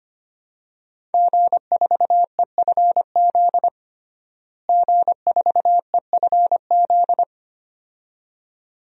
Morse code, heard as a keyed tone, about 25 wpm, 700 Hz. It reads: G4EFZ G4EFZ
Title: DXCC-100 Call Signs - 5 Characters - Repeated 2x - 25wpm